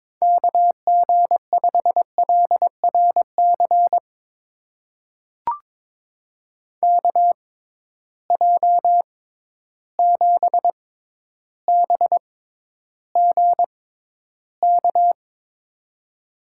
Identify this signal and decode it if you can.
Morse code, heard as a keyed tone, about 21 wpm, spaced as for 11 wpm, 700 Hz. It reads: KG5LRC KJ7BGK
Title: USA Call Signs - 6 Characters - 6x Character Spacing - 22wpm